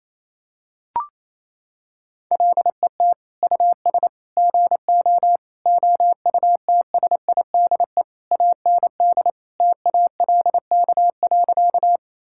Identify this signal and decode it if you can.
Morse code, heard as a keyed tone, about 28 wpm, 700 Hz. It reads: LET US GO OUTSIDE AND TALK.
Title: Sentences from Top 1000 Words - Review Part 6 - 28wpm